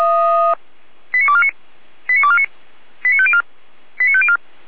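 Five-tone selective call (ZVEI), heard as quick runs of five tones, tones of about 70 ms. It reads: 89249 89249 89504 89504
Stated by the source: ZVEI five-tone selective call received on an SDR, recorded by the feuerwehr-orlamuende.org fire brigade server.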